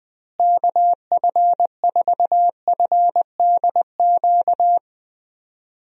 Morse code, heard as a keyed tone, about 21 wpm, 700 Hz. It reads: KF4FDQ
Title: USA Call Signs - 6 Characters - Repeated 3x - 20wpm